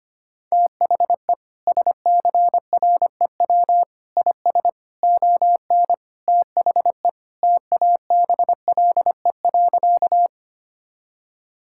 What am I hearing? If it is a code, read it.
Morse code, heard as a keyed tone, about 25 wpm, 700 Hz. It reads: THE SCREW IS ON THE TABLE.
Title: Sentences from Top 1600 Words 25wpm